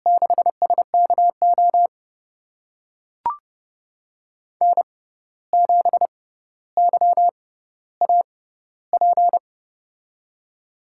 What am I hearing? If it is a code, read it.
Morse code, heard as a keyed tone, about 30 wpm, spaced as for 14 wpm, 700 Hz. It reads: 6SKO N7YAP
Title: USA Call Signs - 6x Character Spacing - 30wpm